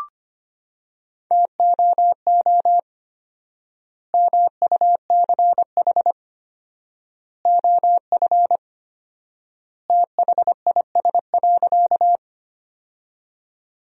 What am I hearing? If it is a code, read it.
Morse code, heard as a keyed tone, about 25 wpm, 700 Hz. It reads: TOO MUCH OF THIS.
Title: 4x Word-Spacing - Sentences from Top 200 Words - 25wpm